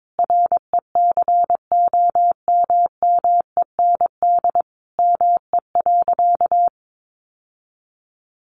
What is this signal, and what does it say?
Morse code, heard as a keyed tone, about 22 wpm, 700 Hz. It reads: RECOMMEND ME.
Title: Sentences from Top 1200 Words 22wpm